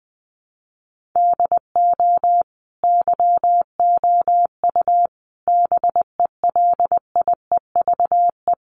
Morse code, heard as a keyed tone, about 20 wpm, 700 Hz. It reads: DO YOU BELIEVE
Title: Sentences from Top 1000 Words 20wpm